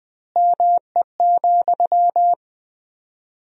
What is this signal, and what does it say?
Morse code, heard as a keyed tone, about 21 wpm, 700 Hz. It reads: ME,
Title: 4x Word-Spacing - Sentences from Top 200 Words - 20wpm